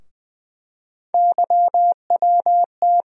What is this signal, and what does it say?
Morse code, heard as a keyed tone, about 21 wpm, 700 Hz. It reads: YWT
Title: USA Call Sign Postfixes - 3 Letters 20wpm